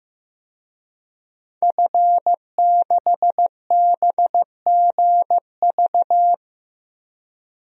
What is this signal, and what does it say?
Morse code, heard as a keyed tone, about 15 wpm, 700 Hz. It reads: F6BGV
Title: DXCC-100 Call Signs - 5 Characters - 15wpm